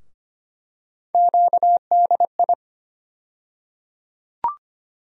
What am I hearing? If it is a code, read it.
Morse code, heard as a keyed tone, about 25 wpm, 700 Hz. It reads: QDI
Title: USA Call Sign Postfixes - 3 Letters 25wpm